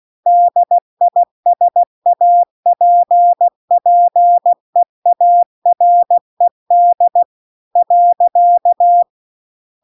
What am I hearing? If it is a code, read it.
Morse code, heard as a keyed tone, about 16 wpm, 700 Hz. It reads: DISAPPEARED .